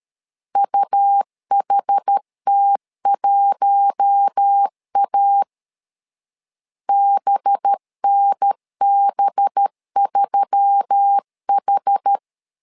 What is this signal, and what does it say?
Morse code, heard as a keyed tone, about 13 wpm, 800 Hz. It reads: UHT1A BNB3H
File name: LEZIONI / 5 LBVC / 5 LB 12wpm 1906